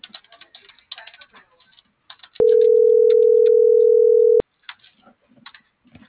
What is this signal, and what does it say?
Ringback tone (North America), 440 + 480 Hz.